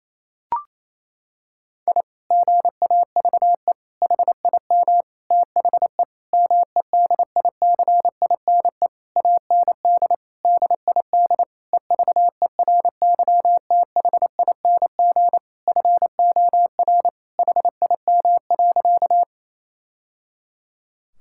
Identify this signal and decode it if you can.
Morse code, heard as a keyed tone, about 28 wpm, 700 Hz. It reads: I GAVE HIM THE MEDICINE AND DID EVERYTHING FOR HIM.
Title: Sentences from Top 1000 Words - Review Part 6 - 28wpm